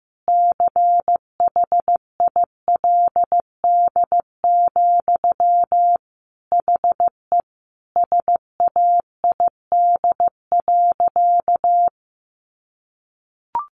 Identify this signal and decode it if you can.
Morse code, heard as a keyed tone, about 14 wpm, 700 Hz. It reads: CHILD, HE SAID.
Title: Sentences from Top 800 Words 15wpm